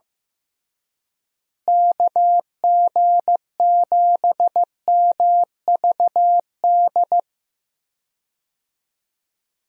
Morse code, heard as a keyed tone, about 15 wpm, 700 Hz. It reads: KG7MVD
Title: USA Call Signs - 6 Characters - Repeated 2x - 15wpm